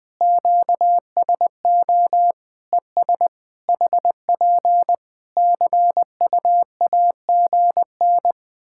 Morse code, heard as a keyed tone, about 20 wpm, 700 Hz. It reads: QSO ES HP CUAGN
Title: Practice QSO - Exchange 4 - Ending - 20wpm